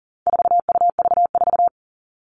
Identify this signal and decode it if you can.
Morse code, heard as a keyed tone, about 40 wpm, 700 Hz. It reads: 4UV4
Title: All Cut Numbers - 4-Digit 40wpm